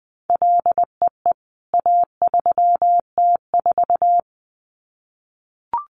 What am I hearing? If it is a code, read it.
Morse code, heard as a keyed tone, about 20 wpm, 700 Hz. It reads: LEE A3T4
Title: CWT Exchange 20wpm